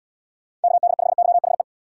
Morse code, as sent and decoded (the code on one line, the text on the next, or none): -.- -.. ..... -.-- .... .
KD5YHE